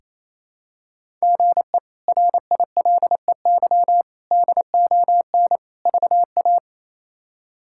Morse code, heard as a keyed tone, about 28 wpm, 700 Hz. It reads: GE RILEY DON VA
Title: K1USN SST QSO Practice - 28wpm